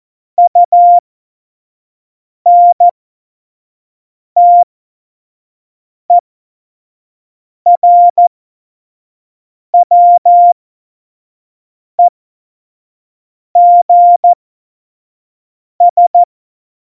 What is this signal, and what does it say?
Morse code, heard as a keyed tone, about 14 wpm, spaced as for 5 wpm, 700 Hz. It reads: UNTERWEGS
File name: Morsekurs / Witze / Witz 26. Februar 2019, 25 F70